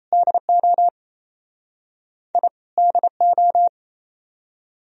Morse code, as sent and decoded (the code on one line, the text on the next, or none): -.. --- / .. / -.. ---
DO I DO